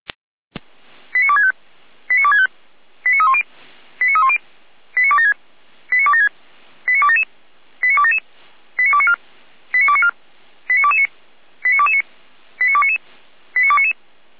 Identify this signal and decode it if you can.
Five-tone selective call (ZVEI), heard as quick runs of five tones, tones of about 70 ms. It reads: 89265 89265 89310 89310 89276 89276 89277 89277 89294 89294 89229 89229 89299 89299